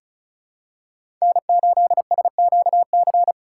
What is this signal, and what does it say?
Morse code, heard as a keyed tone, about 36 wpm, 700 Hz. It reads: N8SQC
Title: USA Call Signs - 5 Characters - 3x Character Spacing - 35wpm